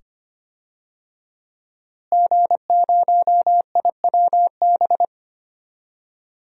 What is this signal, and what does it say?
Morse code, heard as a keyed tone, about 25 wpm, 700 Hz. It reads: G0IWB
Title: DXCC-100 Call Signs - 5 Characters - Repeated 2x - 25wpm